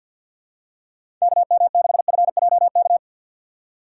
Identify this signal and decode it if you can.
Morse code, heard as a keyed tone, about 53 wpm, 700 Hz. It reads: KM6FJK
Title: USA Call Signs - 4x Character Spacing - 50wpm